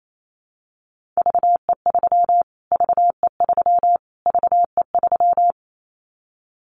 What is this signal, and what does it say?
Morse code, heard as a keyed tone, about 28 wpm, 700 Hz. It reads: VE3 VE3 VE3
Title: Callsign Prefixes for Top 25 DXCC - Repeated 3x - 28wpm